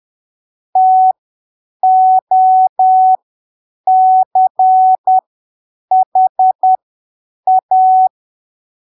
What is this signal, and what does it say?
Morse code, heard as a keyed tone, about 10 wpm, spaced as for 7 wpm, 750 Hz. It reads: TOCHA
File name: arquivos / texto / texto 08 10ppm